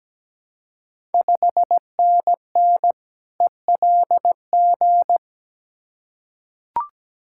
Morse code, heard as a keyed tone, about 17 wpm, 690 Hz. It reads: 5NN ELG
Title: Ontario QSO Party Exchange 17wpm